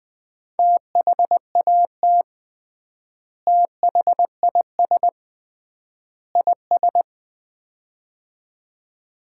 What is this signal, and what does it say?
Morse code, heard as a keyed tone, about 21 wpm, 700 Hz. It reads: THAT THIS IS
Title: 3x Word-Spacing - Sets of 3 Words - 20wpm